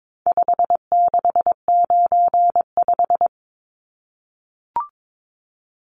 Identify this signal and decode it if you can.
Morse code, heard as a keyed tone, about 22 wpm, 700 Hz. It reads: 5695
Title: Numbers - 4-Digit 22wpm